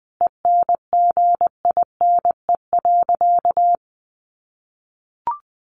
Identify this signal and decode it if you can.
Morse code, heard as a keyed tone, about 20 wpm, 700 Hz. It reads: ENGINE.
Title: Sentences from Top 1500 Words 20wpm